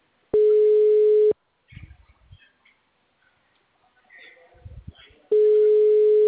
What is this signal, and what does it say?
Ringback tone (Europe, 425 Hz).